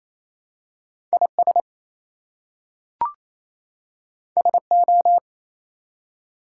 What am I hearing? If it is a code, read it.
Morse code, heard as a keyed tone, about 27 wpm, 700 Hz. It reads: IS SO
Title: Two Letter Words - 28wpm